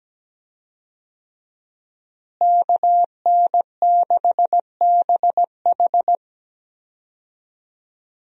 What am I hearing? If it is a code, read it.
Morse code, heard as a keyed tone, about 17 wpm, 700 Hz. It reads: KN6BH
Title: USA Call Signs - 5 Characters - Repeated 2x - 17wpm